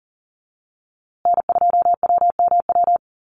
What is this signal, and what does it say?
Morse code, heard as a keyed tone, about 40 wpm, 700 Hz. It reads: N2WMW
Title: USA Call Signs Repeated 2x 40wpm